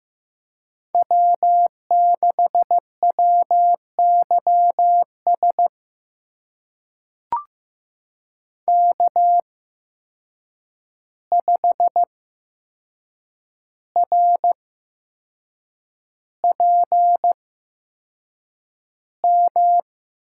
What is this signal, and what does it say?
Morse code, heard as a keyed tone, about 15 wpm, spaced as for 6 wpm, 700 Hz. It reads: W6WYS K5RPM
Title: USA Call Signs - 5 Characters - 8x Character Spacing - 15wpm